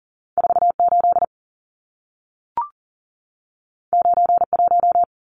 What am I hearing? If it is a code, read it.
Morse code, heard as a keyed tone, about 40 wpm, 700 Hz. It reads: 48 91
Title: Numbers - 2-Digit 40wpm